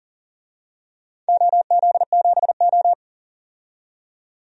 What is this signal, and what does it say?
Morse code, heard as a keyed tone, about 40 wpm, 700 Hz. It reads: OZ7O
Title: DXCC-100 Call Signs - 4 Characters - 40wpm